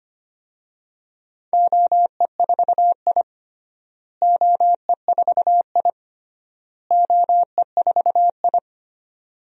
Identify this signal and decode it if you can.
Morse code, heard as a keyed tone, about 25 wpm, 700 Hz. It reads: OE4I OE4I OE4I
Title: DXCC-100 Call Signs - 4 Characters - Repeated 3x - 25wpm